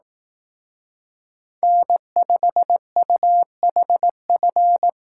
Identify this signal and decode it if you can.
Morse code, heard as a keyed tone, about 18 wpm, 700 Hz. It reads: N5UHF
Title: Speed Racing - USA 1x3 Call Signs 15wpm